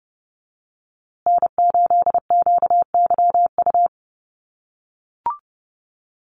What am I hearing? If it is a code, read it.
Morse code, heard as a keyed tone, about 30 wpm, 700 Hz. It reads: N8QYU
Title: USA Call Signs Repeated 2x 30wpm